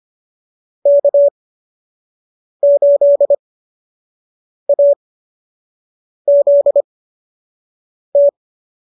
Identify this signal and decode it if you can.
Morse code, heard as a keyed tone, about 25 wpm, spaced as for 6 wpm, 570 Hz. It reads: K8AZT